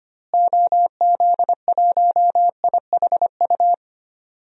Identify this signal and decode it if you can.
Morse code, heard as a keyed tone, about 25 wpm, 700 Hz. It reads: OZ1IHU
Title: DXCC-100 Call Signs - 6 Characters - Repeated 2x - 25wpm